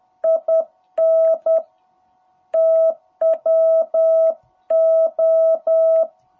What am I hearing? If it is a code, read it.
Morse code, heard as a keyed tone, about 10 wpm, 660 Hz. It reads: IN TWO
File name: jh8cbh / 2021cw